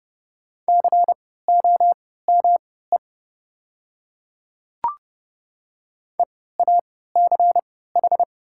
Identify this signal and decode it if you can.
Morse code, heard as a keyed tone, about 30 wpm, spaced as for 17 wpm, 700 Hz. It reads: COME EACH
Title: 3x Character Spacing - Top 200 Words - 30wpm